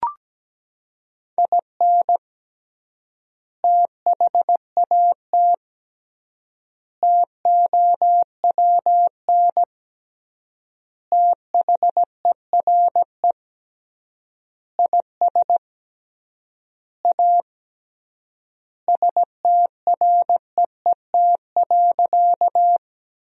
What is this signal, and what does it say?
Morse code, heard as a keyed tone, about 17 wpm, 700 Hz. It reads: IN THAT TOWN THERE IS A STREET.
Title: 3x Word-Spacing - Sentences from Top 500 Words - 17wpm